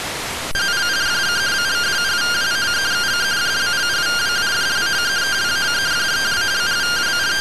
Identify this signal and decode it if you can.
Two-tone teleprinter signal, about 170 Hz shift.